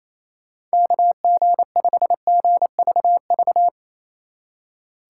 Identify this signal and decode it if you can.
Morse code, heard as a keyed tone, about 34 wpm, 700 Hz. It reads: KG5GVV
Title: Speed Racing - USA 2x3 Call Signs 28wpm